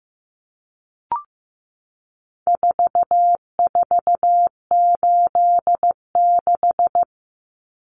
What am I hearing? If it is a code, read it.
Morse code, heard as a keyed tone, about 15 wpm, 700 Hz. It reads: 4486
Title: Numbers - 4-Digit 15wpm